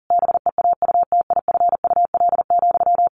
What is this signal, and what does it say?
Morse code, heard as a keyed tone, about 40 wpm, 700 Hz. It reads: BEAUTIFUL,
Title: Sentences from Top 1200 Words 40wpm